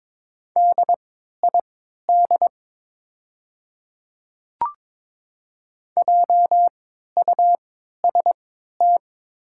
Morse code, heard as a keyed tone, about 22 wpm, spaced as for 13 wpm, 700 Hz. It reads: DID JUST
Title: 3x Character Spacing - Top 200 Words - 22wpm